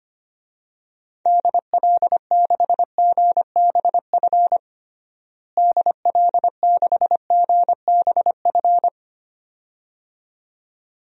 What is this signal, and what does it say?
Morse code, heard as a keyed tone, about 25 wpm, 700 Hz. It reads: DL6GBF DL6GBF
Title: DXCC-100 Call Signs - 6 Characters - Repeated 2x - 25wpm